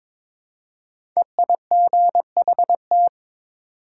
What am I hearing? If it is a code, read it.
Morse code, heard as a keyed tone, about 22 wpm, 700 Hz. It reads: EIGHT
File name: practice / 22wpm / 5 Letter Words - First 3 Letters Given - 22wpm